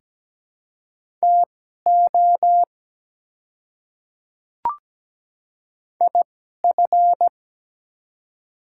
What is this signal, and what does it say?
Morse code, heard as a keyed tone, about 17 wpm, spaced as for 12 wpm, 690 Hz. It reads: TO IF
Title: 2x Character Spacing - Top 100 Words - 17wpm